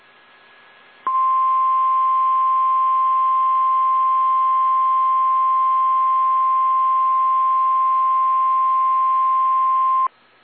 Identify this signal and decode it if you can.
Emergency Alert System (SAME) tones (the 1050 Hz attention tone). It is heard from 1.0 to 10.1 s.
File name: eas / KSDZKDJL / KSDZKDJL